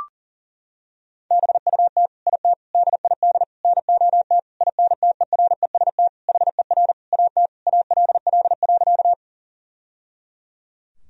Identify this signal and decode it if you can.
Morse code, heard as a keyed tone, about 37 wpm, 700 Hz. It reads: BUT IT DID NOT INTEREST HER AT ALL.
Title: Sentences from Top 1000 Words - Review Part 2 - 40wpm